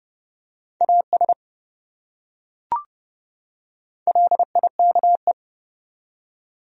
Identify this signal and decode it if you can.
Morse code, heard as a keyed tone, about 28 wpm, 700 Hz. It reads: AS LIKE